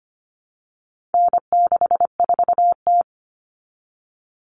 Morse code, heard as a keyed tone, about 25 wpm, 700 Hz. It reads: N64T
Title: Common Cut Numbers - 4-Digit 25wpm